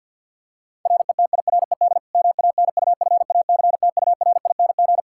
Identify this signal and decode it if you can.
Morse code, heard as a keyed tone, about 47 wpm, 700 Hz. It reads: RETIRED MANUFACTURING
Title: Practice QSO - Exchange 3 - More Info - 50wpm